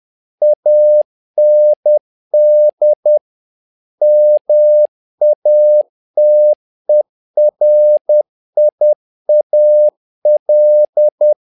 Morse code, heard as a keyed tone, about 10 wpm, 600 Hz. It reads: AND MATERIAL
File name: learning center / tower trouble / tower trouble 10wpm